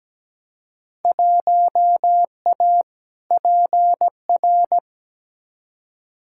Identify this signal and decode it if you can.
Morse code, heard as a keyed tone, about 17 wpm, 700 Hz. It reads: 1A PR
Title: Field Day Exchanges Encore - 17wpm